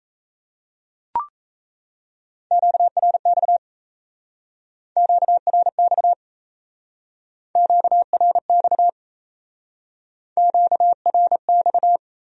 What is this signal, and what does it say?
Morse code, heard as a keyed tone, about 35 wpm, 700 Hz. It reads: QRX QRX QRX QRX
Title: Speed-Racing Q-Codes - 28wpm